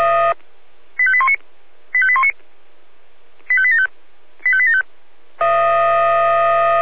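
Five-tone selective call (ZVEI), heard as quick runs of five tones, tones of about 70 ms. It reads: 85819 85819 85875 85875
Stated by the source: ZVEI five-tone selective call received on SDR.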